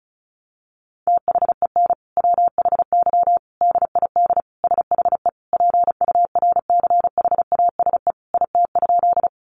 Morse code, heard as a keyed tone, about 34 wpm, 700 Hz. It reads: THEN WHY DID SHE PURCHASE IT?
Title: Sentences from Top 800 Words 35wpm